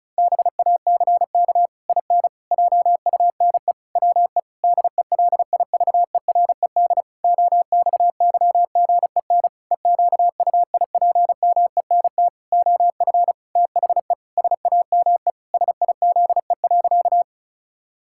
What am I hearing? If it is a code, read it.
Morse code, heard as a keyed tone, about 36 wpm, 700 Hz. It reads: BACK IN JUNE WE DELIVERED OXYGEN EQUIPMENT OF THE SAME SIZE.
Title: Pangram Challenge v2 - 35wpm